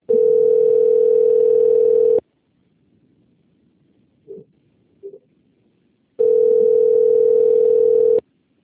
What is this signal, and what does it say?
Ringback tone (North America), 440 + 480 Hz.
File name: out- - - - - .54433